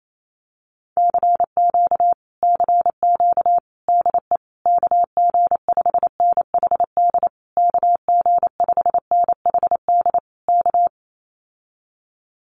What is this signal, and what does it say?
Morse code, heard as a keyed tone, about 28 wpm, 700 Hz. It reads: CQ CQ DE KG5NHD KG5NHD K
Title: Common QSO Phrases Encore 28wpm